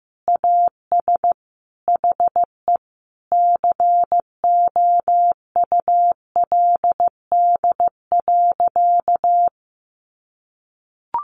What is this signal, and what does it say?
Morse code, heard as a keyed tone, about 15 wpm, 700 Hz. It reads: AS HE COULD.
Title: Sentences from Top 500 Words 15wpm